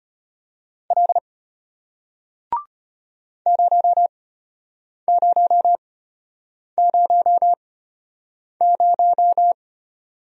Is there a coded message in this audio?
Yes: Morse code from 0.9 to 9.5 s, heard as a keyed tone, about 33 wpm, spaced as for 8 wpm, 700 Hz. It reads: L 0000